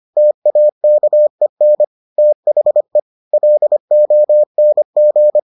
Morse code, heard as a keyed tone, about 25 wpm, 600 Hz. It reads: TAKEN THE LONG